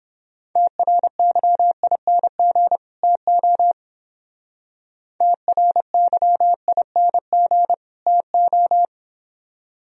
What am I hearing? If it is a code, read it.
Morse code, heard as a keyed tone, about 30 wpm, 700 Hz. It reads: TRYING TO TRYING TO